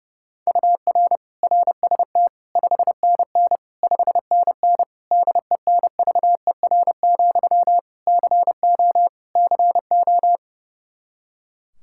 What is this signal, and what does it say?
Morse code, heard as a keyed tone, about 30 wpm, 700 Hz. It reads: UR RST 5NN 5NN DENVER, CO CO
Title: Practice QSO - Exchange 1 - Beginning - 30wpm